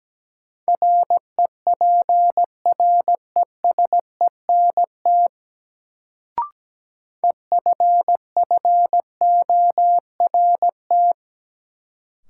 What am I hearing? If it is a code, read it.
Morse code, heard as a keyed tone, about 17 wpm, 700 Hz. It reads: REPRESENT EFFORT
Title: Sentences from Top 1200 Words - New Word Drill - 17wpm